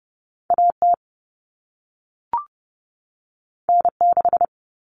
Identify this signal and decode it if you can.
Morse code, heard as a keyed tone, about 30 wpm, 700 Hz. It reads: AT N6
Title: Common Cut Numbers - 2-Digit 30wpm